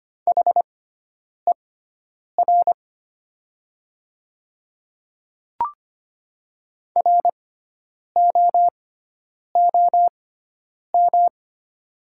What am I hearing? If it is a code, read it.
Morse code, heard as a keyed tone, about 25 wpm, spaced as for 9 wpm, 700 Hz. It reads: HER ROOM
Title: 6x Character Spacing - Top 300 Words - 25wpm